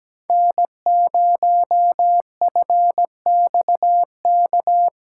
Morse code, heard as a keyed tone, about 17 wpm, 700 Hz. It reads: N0FXK
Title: USA Callsign DE Callsign - 5 Characters - 17wpm